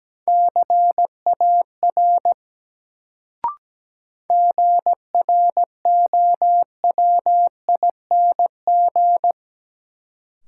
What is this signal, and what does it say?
Morse code, heard as a keyed tone, about 17 wpm, 700 Hz. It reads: CAR GROWING
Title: Sets of 5 Words Encore - New Word Drill - 17wpm